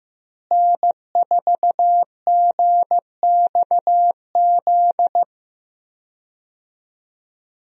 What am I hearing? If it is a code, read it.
Morse code, heard as a keyed tone, about 15 wpm, 700 Hz. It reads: N4GXZ
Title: USA Call Signs - 5 Characters - Repeated 3x - 15wpm